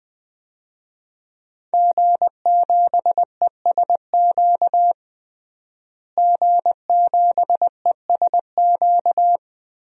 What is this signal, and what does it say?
Morse code, heard as a keyed tone, about 20 wpm, 700 Hz. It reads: G7ESQ G7ESQ
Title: DXCC-100 Call Signs - 5 Characters - Repeated 3x - 20wpm